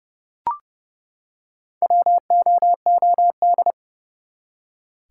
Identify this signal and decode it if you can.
Morse code, heard as a keyed tone, about 30 wpm, 700 Hz. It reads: WOOD
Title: Four Letter Words - 30wpm